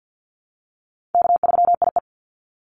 Morse code, heard as a keyed tone, about 49 wpm, 700 Hz. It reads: K3IE
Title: Easy Call Signs Repeated 3x - 50wpm